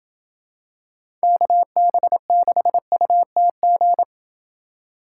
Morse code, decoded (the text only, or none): KB6UTG